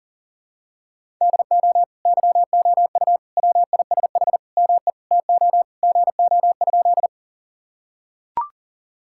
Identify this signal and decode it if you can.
Morse code, heard as a keyed tone, about 37 wpm, 700 Hz. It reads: DO YOU WISH ME TO GO?
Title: Sentences from Top 1600 Words 40wpm